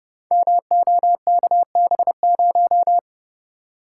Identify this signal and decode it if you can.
Morse code, heard as a keyed tone, about 30 wpm, 700 Hz. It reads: MOKB0
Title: Code Groups - 5 Letter:Numbers - 30wpm